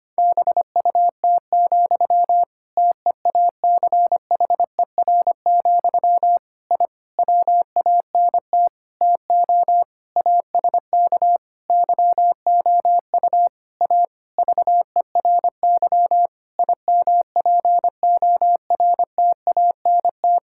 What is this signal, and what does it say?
Morse code, heard as a keyed tone, about 25 wpm, 700 Hz. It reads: BUT, TEACHER, I WANT TO ASK YOU A VERY IMPORTANT
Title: Sentences from Top 1000 Words - Review Part 6 - 25wpm